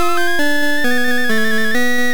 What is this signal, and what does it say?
There is no code.